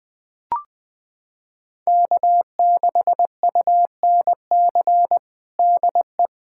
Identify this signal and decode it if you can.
Morse code, heard as a keyed tone, about 20 wpm, 700 Hz. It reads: K6UNC DE
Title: USA Callsign DE Callsign - 5 Characters - 20wpm